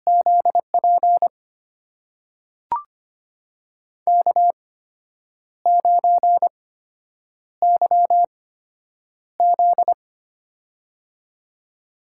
Morse code, heard as a keyed tone, about 25 wpm, spaced as for 8 wpm, 700 Hz. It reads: ZP K9YZ
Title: USA Call Signs - 4 Characters - 8x Character Spacing - 25wpm